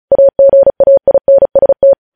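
Morse code, heard as a keyed tone, about 36 wpm, 550 Hz. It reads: AGAINST
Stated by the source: News headlines sent in Morse at speed 35.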